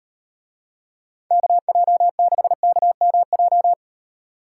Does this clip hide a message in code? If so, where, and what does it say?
Yes: Morse code from 1.3 to 3.7 s, heard as a keyed tone, about 38 wpm, 700 Hz. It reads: KJ6KMJ